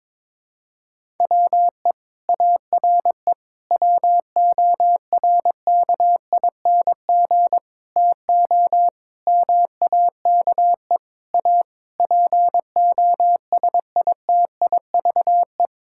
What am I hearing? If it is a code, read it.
Morse code, heard as a keyed tone, about 22 wpm, 700 Hz. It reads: WE ARE WORKING TO MAKE A POSITIVE
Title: Sentences from Top 1000 Words - Review Part 6 - 22wpm